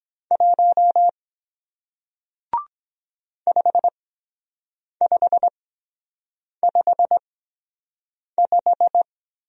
Morse code, heard as a keyed tone, about 23 wpm, spaced as for 7 wpm, 700 Hz. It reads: 1 5555